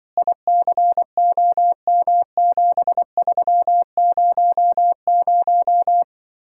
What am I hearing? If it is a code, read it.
Morse code, heard as a keyed tone, about 23 wpm, 700 Hz. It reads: ICOM7300
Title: Speed-Racing Common QSO Phrases Encore - 20wpm